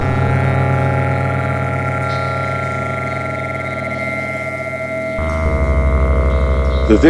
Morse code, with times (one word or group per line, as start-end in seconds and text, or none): none